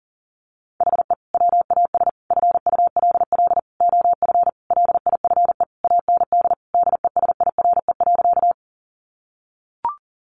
Morse code, heard as a keyed tone, about 40 wpm, 700 Hz. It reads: HE WAS FULL OF LIFE AND DESIRE.
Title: Sentences from Top 1100 Words 40wpm